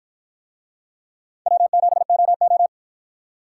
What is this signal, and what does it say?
Morse code, heard as a keyed tone, about 53 wpm, 700 Hz. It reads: W7OO